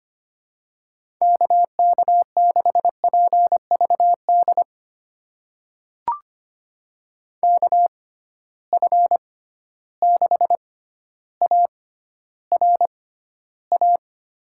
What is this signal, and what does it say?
Morse code, heard as a keyed tone, about 25 wpm, spaced as for 13 wpm, 700 Hz. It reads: KK6PVD KF6ARA
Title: USA Call Signs - 6 Characters - 6x Character Spacing - 25wpm